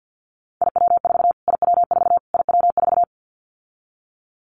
Morse code, heard as a keyed tone, about 48 wpm, 700 Hz. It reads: IW4 IW4 IW4
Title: Callsign Prefixes for Top 25 DXCC - Repeated 3x - 50wpm